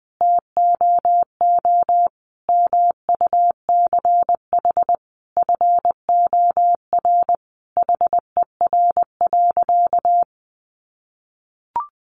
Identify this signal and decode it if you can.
Morse code, heard as a keyed tone, about 20 wpm, 700 Hz. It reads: TOO MUCH FOR HER.
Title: Sentences from Top 1200 Words 20wpm